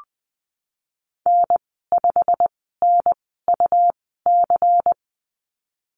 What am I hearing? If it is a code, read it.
Morse code, heard as a keyed tone, about 20 wpm, spaced as for 15 wpm, 700 Hz. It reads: N5NUC
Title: 122 Lesson 20g - Number 3 - Call Signs 20wpm